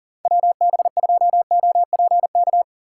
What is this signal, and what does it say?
Morse code, heard as a keyed tone, about 48 wpm, 700 Hz. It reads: WB2OPK
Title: Speed Racing - USA Call Signs 40wpm